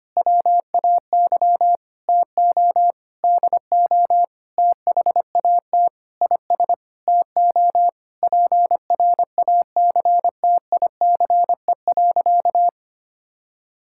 Morse code, heard as a keyed tone, about 25 wpm, 700 Hz. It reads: WAY TO DO THAT IS TO PRACTICE.
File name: practice / 25wpm / Sentences with 9-Word Phrases - 25wpm